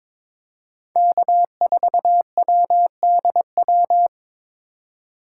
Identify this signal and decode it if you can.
Morse code, heard as a keyed tone, about 22 wpm, 700 Hz. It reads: K4WDW
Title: USA Call Signs - 5 Characters - 3x Character Spacing - 22wpm